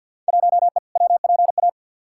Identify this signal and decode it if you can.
Morse code, heard as a keyed tone, about 53 wpm, 700 Hz. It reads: 1E WPA
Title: Field Day Exchanges - 50wpm